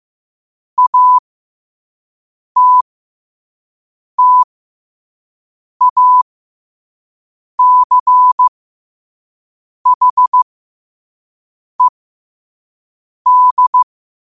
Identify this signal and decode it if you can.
Morse code, heard as a keyed tone, about 15 wpm, spaced as for 5 wpm, 1000 Hz. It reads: ATTACHED
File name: qotdfiles / 5wpm